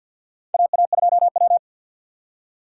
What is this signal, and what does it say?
Morse code, heard as a keyed tone, about 53 wpm, 700 Hz. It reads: AA2W
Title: USA Call Signs - 4 Characters - 3x Character Spacing - 50wpm